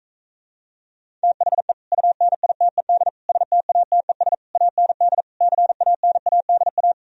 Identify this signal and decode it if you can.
Morse code, heard as a keyed tone, about 45 wpm, 700 Hz. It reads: THE UNITED STATES AND CANADA